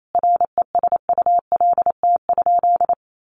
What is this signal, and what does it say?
Morse code, heard as a keyed tone, about 28 wpm, 700 Hz. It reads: RESULT?
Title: Sentences from Top 700 Words 28wpm